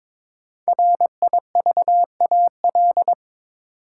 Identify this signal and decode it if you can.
Morse code, heard as a keyed tone, about 23 wpm, 700 Hz. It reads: RIVAL
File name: practice / 22wpm / 5 Letter Words - First 3 Letters Given - 22wpm